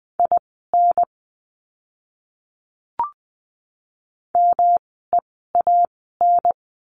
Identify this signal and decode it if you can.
Morse code, heard as a keyed tone, about 20 wpm, spaced as for 14 wpm, 700 Hz. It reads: IN MEAN
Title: 108 Lesson 18f - Letter m - Words 20wpm